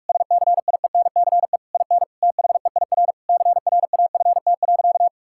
Morse code, heard as a keyed tone, about 42 wpm, 690 Hz. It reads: SCIENCE IN THEIR CRAFT.
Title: Sentences from Top 1000 Words - Review Part 6 - 45wpm